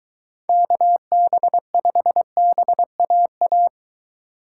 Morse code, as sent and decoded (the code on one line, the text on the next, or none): -.- -... ..... -... .- .-
KB5BAA